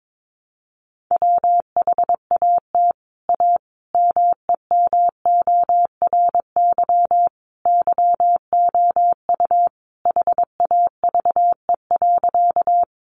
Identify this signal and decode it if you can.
Morse code, heard as a keyed tone, about 22 wpm, 700 Hz. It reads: WHAT A MEMORY YOU HAVE.
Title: Sentences from Top 1300 Words 22wpm